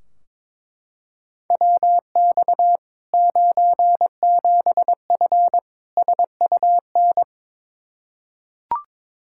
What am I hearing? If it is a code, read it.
Morse code, heard as a keyed tone, about 22 wpm, 700 Hz. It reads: WX 97F SUN
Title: Practice QSO - Exchange 2 - Info - 22wpm